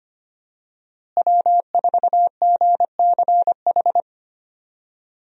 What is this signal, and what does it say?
Morse code, heard as a keyed tone, about 25 wpm, 700 Hz. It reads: W4GCH
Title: USA Call Signs - 5 Characters - Repeated 3x - 25wpm